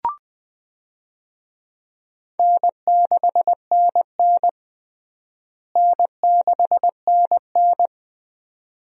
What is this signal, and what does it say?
Morse code, heard as a keyed tone, about 20 wpm, 700 Hz. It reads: N6NN N6NN
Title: USA Call Signs - 4 Characters - Repeated 2x - 20wpm